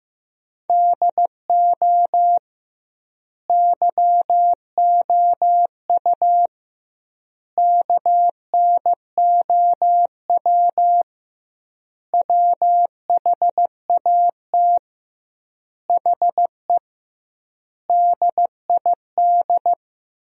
Morse code, heard as a keyed tone, about 16 wpm, 700 Hz. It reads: DO YOU KNOW WHAT HE DID
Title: 2x Word-Spacing - Sentences from Top 300 Words - 15wpm